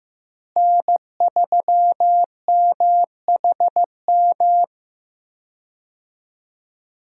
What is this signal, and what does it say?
Morse code, heard as a keyed tone, about 15 wpm, 700 Hz. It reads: N3MHM
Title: USA Call Signs - 5 Characters - Repeated 3x - 15wpm